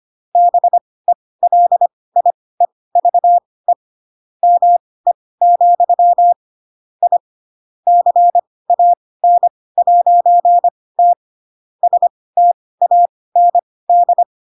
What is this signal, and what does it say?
Morse code, heard as a keyed tone, about 25 wpm, spaced as for 18 wpm, 700 Hz. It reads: BELIEVE ME, I CAN'T STAND
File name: Pinocchio / Pinocchio ch7 12 25 ik1hge